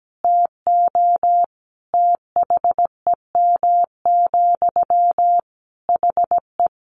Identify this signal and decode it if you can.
Morse code, heard as a keyed tone, about 17 wpm, 700 Hz. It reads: TO THEM, HE
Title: Sentences from Top 400 Words 17wpm